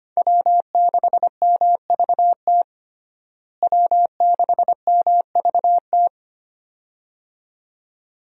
Morse code, heard as a keyed tone, about 25 wpm, 700 Hz. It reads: W6MVT W6MVT
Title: USA Call Signs - 5 Characters - Repeated 2x - 25wpm